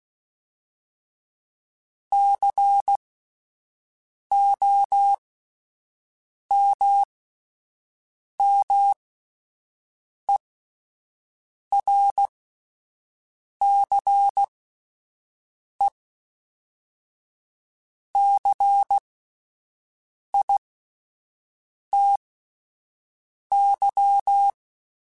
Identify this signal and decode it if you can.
Morse code, heard as a keyed tone, about 16 wpm, spaced as for 5 wpm, 800 Hz. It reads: COMMERCECITY
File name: fromSMs / FromSMs mp3s / Cities@5